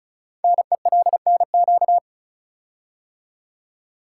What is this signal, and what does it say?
Morse code, heard as a keyed tone, about 36 wpm, 700 Hz. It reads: NELNQ
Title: Code Groups - 5 Letters - 35wpm